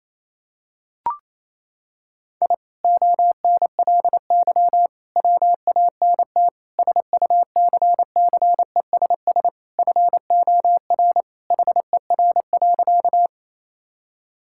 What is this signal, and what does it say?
Morse code, heard as a keyed tone, about 28 wpm, 700 Hz. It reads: I ONLY WANT SUCCESS FOR HER.
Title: Sentences from Top 1000 Words - Review Part 6 - 28wpm